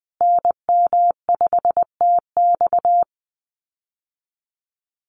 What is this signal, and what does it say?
Morse code, heard as a keyed tone, about 20 wpm, 700 Hz. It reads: NM5TX
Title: USA Call Signs Repeated 2x 20wpm